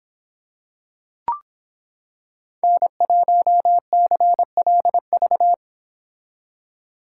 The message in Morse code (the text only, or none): N1CLV